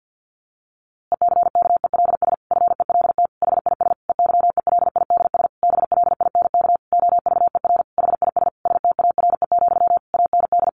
Morse code, heard as a keyed tone, about 50 wpm, 700 Hz. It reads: ECKELS FELT HIS EYELIDS BLINK OVER HIS STARE, AND
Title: A Sound Of Thunder 50wpm